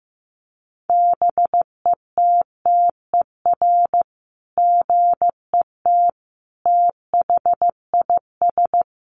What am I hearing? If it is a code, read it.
Morse code, heard as a keyed tone, about 15 wpm, 700 Hz. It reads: BETTER GET THIS
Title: Sentences from Top 600 Words 15wpm